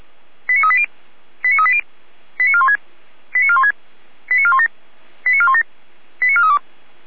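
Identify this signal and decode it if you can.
Five-tone selective call (ZVEI), heard as quick runs of five tones, tones of about 70 ms. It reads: 89380 89380 89416 89416 89417 89417 89432
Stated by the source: ZVEI five-tone selective call 89175, received by SDR.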